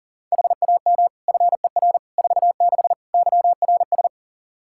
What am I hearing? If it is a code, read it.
Morse code, heard as a keyed tone, about 40 wpm, 700 Hz. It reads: HAM FER 46 YRS
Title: Practice QSO - Exchange 3 - More Info - 40wpm